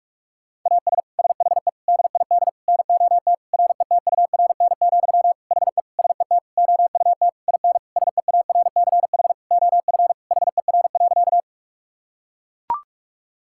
Morse code, heard as a keyed tone, about 41 wpm, 700 Hz. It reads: AS SHE DID NOT RETURN, HE SET OUT IN SEARCH OF HER.
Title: Sentences from Top 1000 Words - Review Part 5 - 45wpm